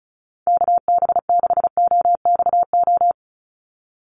Morse code, heard as a keyed tone, about 34 wpm, 700 Hz. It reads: KB6OXO
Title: USA Call Signs 35wpm